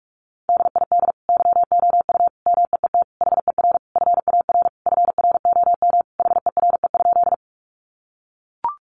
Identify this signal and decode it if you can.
Morse code, heard as a keyed tone, about 44 wpm, 700 Hz. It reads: DID YOU MEET HER FAR FROM HERE?
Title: Sentences from Top 1100 Words 45wpm